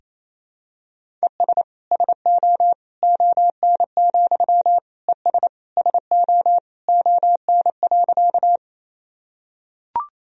Morse code, heard as a keyed tone, about 28 wpm, 700 Hz. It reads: ES SO ON, ES SO ON.
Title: Sentences from Top 100 Words and Abbreviations 28wpm